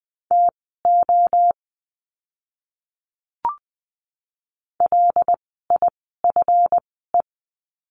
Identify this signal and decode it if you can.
Morse code, heard as a keyed tone, about 20 wpm, spaced as for 14 wpm, 700 Hz. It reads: TO LIFE
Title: 230 Lesson 37f - Letter x - Words 20wpm